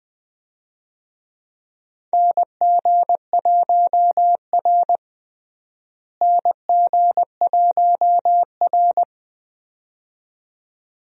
Morse code, heard as a keyed tone, about 20 wpm, 700 Hz. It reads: NG1R NG1R
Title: USA Call Signs - 4 Characters - Repeated 2x - 20wpm